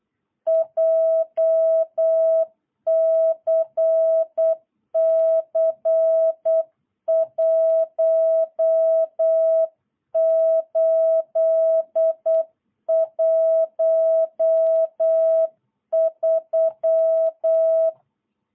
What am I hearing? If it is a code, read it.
Morse code, heard as a keyed tone, about 8 wpm, 650 Hz. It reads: JCC 1813
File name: jh8cbh / 2022cw